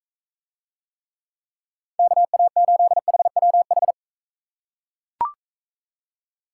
Morse code, heard as a keyed tone, about 44 wpm, 700 Hz. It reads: KA8HWH